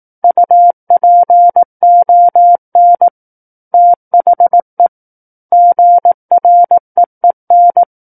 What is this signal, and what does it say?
Morse code, heard as a keyed tone, about 18 wpm, 700 Hz. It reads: UPON THE GREEN
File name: 18 WPM